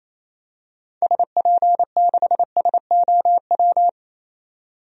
Morse code, heard as a keyed tone, about 28 wpm, 700 Hz. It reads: SP6SOW
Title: DXCC-100 Call Signs - 6 Characters - 28wpm